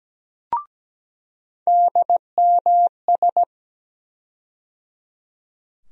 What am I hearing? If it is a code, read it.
Morse code, heard as a keyed tone, about 17 wpm, 700 Hz. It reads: DMS
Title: USA Call Sign Postfixes - 3 Letters 17wpm